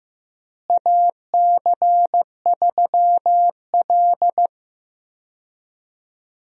Morse code, heard as a keyed tone, about 15 wpm, 700 Hz. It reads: AC3L
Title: USA Call Signs - 4 Characters - Repeated 3x - 15wpm